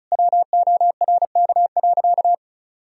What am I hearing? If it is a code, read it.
Morse code, heard as a keyed tone, about 36 wpm, 700 Hz. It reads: WORK.
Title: Sentences from Top 300 Words and Abbreviations 35wpm